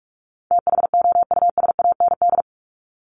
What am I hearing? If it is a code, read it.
Morse code, heard as a keyed tone, about 44 wpm, 700 Hz. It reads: THOUSAND
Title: Top 500 Words 45wpm